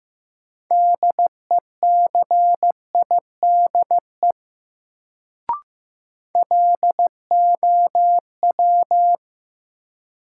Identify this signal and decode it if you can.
Morse code, heard as a keyed tone, about 15 wpm, 700 Hz. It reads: DECIDE LOW
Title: Sentences from Top 500 Words - New Word Drill - 15wpm